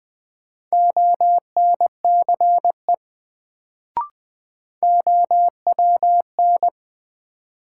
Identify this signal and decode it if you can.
Morse code, heard as a keyed tone, about 21 wpm, 700 Hz. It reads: ONCE OWN
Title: Sentences from Top 300 Words - New Word Drill - 20wpm